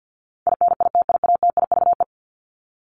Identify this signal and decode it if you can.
Morse code, heard as a keyed tone, about 50 wpm, 700 Hz. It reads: INITIATIVE
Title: Top 1751-2000 Words 50wpm